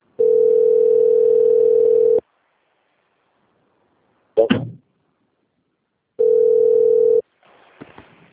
Ringback tone (North America), 440 + 480 Hz.